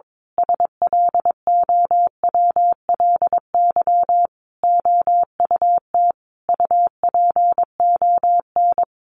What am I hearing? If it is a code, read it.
Morse code, heard as a keyed tone, about 22 wpm, 700 Hz. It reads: SLOWLY OUT UPON
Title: A Sound Of Thunder 22wpm